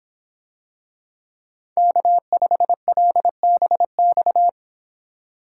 Morse code, heard as a keyed tone, about 24 wpm, 700 Hz. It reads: K5LBX